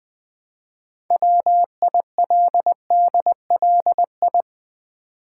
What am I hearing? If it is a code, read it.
Morse code, heard as a keyed tone, about 20 wpm, 690 Hz. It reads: WILDLI
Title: Build A Word - 3 Letters At A Time - 8-Letter Words - 20wpm